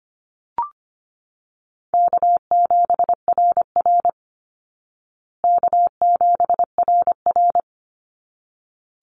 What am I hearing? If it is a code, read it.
Morse code, heard as a keyed tone, about 25 wpm, 700 Hz. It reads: K7RR K7RR
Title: Easy Call Signs Repeated 2x - 25wpm